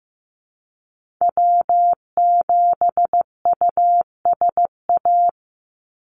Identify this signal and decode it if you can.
Morse code, heard as a keyed tone, about 15 wpm, 700 Hz. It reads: W7USA
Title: NA SOTA Activator Callsigns 15wpm